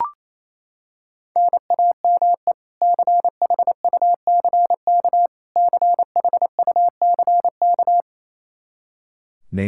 Morse code, heard as a keyed tone, about 28 wpm, 700 Hz. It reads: NAME CHUCK CHUCK
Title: K1USN SST QSO Practice 28wpm